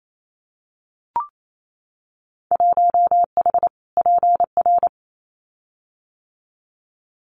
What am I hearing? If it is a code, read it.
Morse code, heard as a keyed tone, about 28 wpm, 690 Hz. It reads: 1H PR
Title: Winter Field Day Exchanges Encore 28wpm